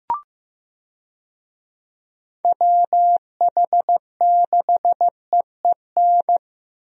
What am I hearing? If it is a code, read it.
Morse code, heard as a keyed tone, about 15 wpm, 700 Hz. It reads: WH6EEN
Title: USA Call Signs - 6 Characters - Repeated 2x - 15wpm